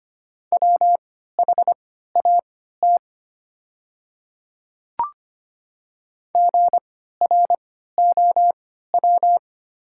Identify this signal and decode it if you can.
Morse code, heard as a keyed tone, about 25 wpm, spaced as for 14 wpm, 700 Hz. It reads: WHAT GROW
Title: 3x Character Spacing - Top 300 Words - 25wpm